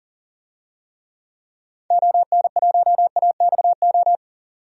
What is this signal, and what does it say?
Morse code, heard as a keyed tone, about 40 wpm, 700 Hz. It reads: ON1AXO
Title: DXCC-100 Call Signs - 6 Characters - 40wpm